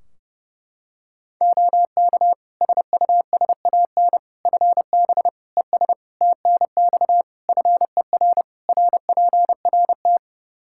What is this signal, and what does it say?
Morse code, heard as a keyed tone, about 32 wpm, 700 Hz. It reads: OK SUSAN FB ES TNX FER RPRT
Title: Practice QSO - Exchange 2 - Info - 30wpm